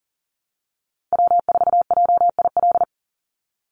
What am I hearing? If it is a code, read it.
Morse code, heard as a keyed tone, about 41 wpm, 700 Hz. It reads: W4JIL
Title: USA Call Signs 40wpm